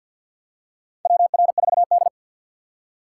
Morse code, heard as a keyed tone, about 52 wpm, 700 Hz. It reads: WR4D
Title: USA Call Signs - 4 Characters - 2x Character Spacing - 50wpm